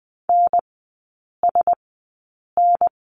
Morse code, heard as a keyed tone, about 20 wpm, spaced as for 8 wpm, 700 Hz. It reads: NSN
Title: 047 Lesson 9e - Number 4 - Three Characters 20wpm